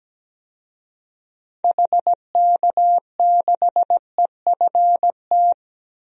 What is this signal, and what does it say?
Morse code, heard as a keyed tone, about 17 wpm, 700 Hz. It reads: HK6EFT
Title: DXCC-100 Call Signs - 6 Characters - 17wpm